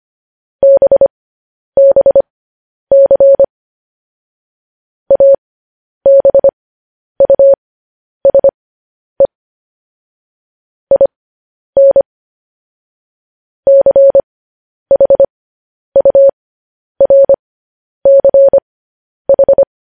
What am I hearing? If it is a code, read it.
Morse code, heard as a keyed tone, about 25 wpm, spaced as for 10 wpm, 550 Hz. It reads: BBC ABUSE IN CHURCH
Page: news-headlines-in-morse-code-at-10-wpm